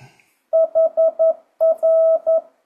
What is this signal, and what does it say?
Morse code, heard as a keyed tone, about 11 wpm, 650 Hz. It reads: HR
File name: 2025cw / speedup / 11wpm